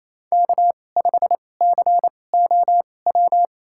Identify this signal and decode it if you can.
Morse code, heard as a keyed tone, about 27 wpm, spaced as for 20 wpm, 700 Hz. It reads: K5COW
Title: USA Call Signs - 2x Character Spacing - 28wpm